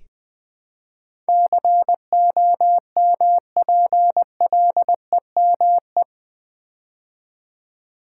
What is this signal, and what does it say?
Morse code, heard as a keyed tone, about 20 wpm, 700 Hz. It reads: COMPLEME
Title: Build A Word - 4 Letters At A Time - 10-Letter Words - 20wpm